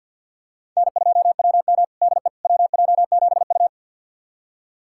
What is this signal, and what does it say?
Morse code, heard as a keyed tone, about 50 wpm, 690 Hz. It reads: N2WM DE WJ7U
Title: USA Callsign DE Callsign - 4 Characters - 50wpm